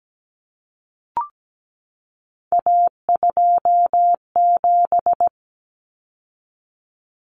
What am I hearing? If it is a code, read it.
Morse code, heard as a keyed tone, about 17 wpm, 700 Hz. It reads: A27
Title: Common Cut Numbers - 3-Digit 17wpm